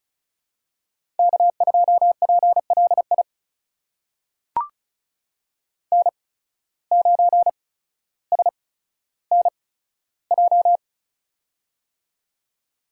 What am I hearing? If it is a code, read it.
Morse code, heard as a keyed tone, about 36 wpm, spaced as for 15 wpm, 700 Hz. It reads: K2PLI N9SNJ
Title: USA Call Signs - 5 Characters - 8x Character Spacing - 35wpm